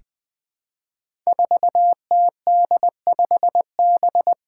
Morse code, heard as a keyed tone, about 20 wpm, 700 Hz. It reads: 4TD5B
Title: Five Challenging Letter-Number Combinations 20wpm